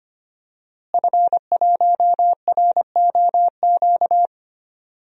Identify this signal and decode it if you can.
Morse code, heard as a keyed tone, about 25 wpm, 700 Hz. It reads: F1ROQ
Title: DXCC-100 Call Signs - 5 Characters - Repeated 3x - 25wpm